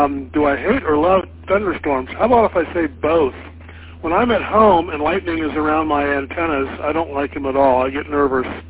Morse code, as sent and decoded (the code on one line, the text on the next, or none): none